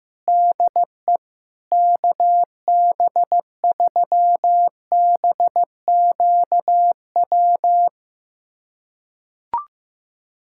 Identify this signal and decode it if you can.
Morse code, heard as a keyed tone, about 16 wpm, 700 Hz. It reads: DE KB3BQW
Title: USA Callsign DE Callsign - 6 Characters - 15wpm